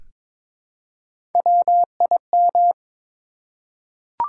Morse code, heard as a keyed tone, about 22 wpm, 700 Hz. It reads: WIM
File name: practice / 22wpm / Three Letter-Number Combinations - 22wpm